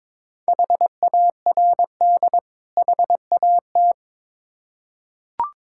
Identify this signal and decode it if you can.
Morse code, heard as a keyed tone, about 22 wpm, 690 Hz. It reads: HARD HAT
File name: Difficult Words with Hints - 22wpm